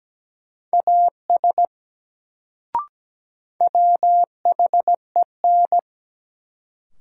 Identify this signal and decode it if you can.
Morse code, heard as a keyed tone, about 17 wpm, 700 Hz. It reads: AS WHEN
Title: Sentences from Top 100 Words - New Word Drill - 17wpm